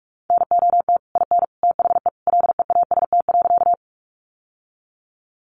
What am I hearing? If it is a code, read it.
Morse code, heard as a keyed tone, about 44 wpm, 700 Hz. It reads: NOT IN THE LEAST.
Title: Sentences from Top 700 Words 45wpm